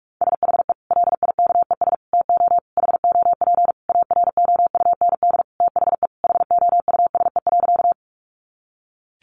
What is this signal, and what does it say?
Morse code, heard as a keyed tone, about 44 wpm, 700 Hz. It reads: SHE LIKES TO HOP AROUND THE HOUSE.
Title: Easy Phrases 45wpm